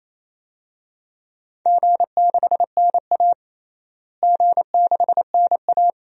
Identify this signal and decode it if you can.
Morse code, heard as a keyed tone, about 28 wpm, 700 Hz. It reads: G6NA G6NA
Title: DXCC-100 Call Signs - 4 Characters - Repeated 2x - 28wpm